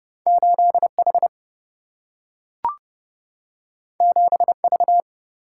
Morse code, heard as a keyed tone, about 31 wpm, 700 Hz. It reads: 8H 7V
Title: Single-Letter Single-Number 30wpm